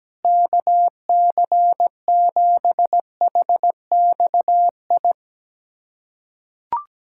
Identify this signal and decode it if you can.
Morse code, heard as a keyed tone, about 17 wpm, 700 Hz. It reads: KC7HXI
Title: USA Callsign DE Callsign - 6 Characters - 17wpm